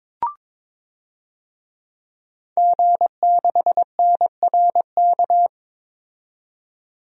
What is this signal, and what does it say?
Morse code, heard as a keyed tone, about 22 wpm, 700 Hz. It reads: G6NRK
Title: DXCC-100 Call Signs - 5 Characters - 22wpm